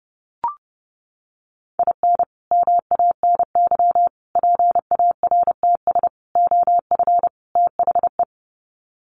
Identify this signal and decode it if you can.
Morse code, heard as a keyed tone, about 30 wpm, 700 Hz. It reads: IN MANY PARTS OF THE
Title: Sets of 5 Words Encore 30wpm